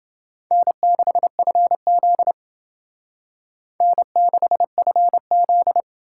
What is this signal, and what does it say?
Morse code, heard as a keyed tone, about 30 wpm, 700 Hz. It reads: N6FZ N6FZ